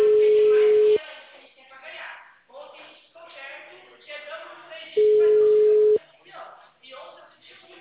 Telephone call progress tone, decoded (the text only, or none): ringback tone